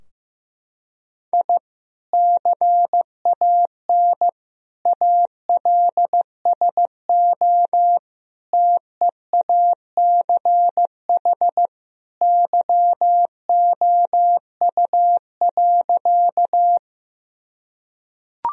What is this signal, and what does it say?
Morse code, heard as a keyed tone, about 15 wpm, 700 Hz. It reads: I CAN ALSO TEACH YOU.
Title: Sentences from Top 1000 Words - Review Part 2 - 15wpm